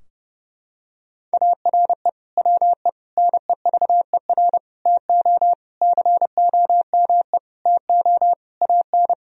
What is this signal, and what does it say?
Morse code, heard as a keyed tone, about 32 wpm, 700 Hz. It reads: ARE WE NEVER TO COME TO AN
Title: Sentences from Top 1000 Words - Review Part 3 - 30wpm